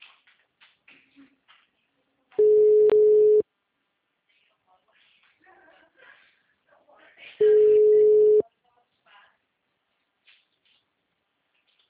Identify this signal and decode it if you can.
Ringback tone (Europe, 425 Hz), with speech.